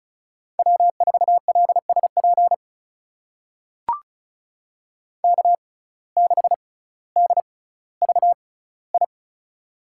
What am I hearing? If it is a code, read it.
Morse code, heard as a keyed tone, about 36 wpm, spaced as for 18 wpm, 700 Hz. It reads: W4LSP K6DVI
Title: USA Call Signs - 5 Characters - 6x Character Spacing - 35wpm